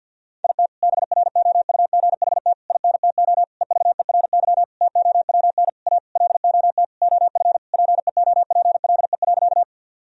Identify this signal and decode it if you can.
Morse code, heard as a keyed tone, about 50 wpm, 700 Hz. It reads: IT BROUGHT INTO EVERY TOWN A LOT OF PEOPLE.
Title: Sentences from Top 1000 Words - Review Part 3 - 50wpm